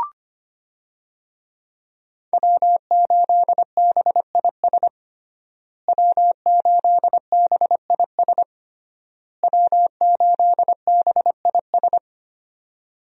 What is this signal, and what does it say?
Morse code, heard as a keyed tone, about 25 wpm, 700 Hz. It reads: W8BIS W8BIS W8BIS
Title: USA Call Signs - 5 Characters - Repeated 3x - 25wpm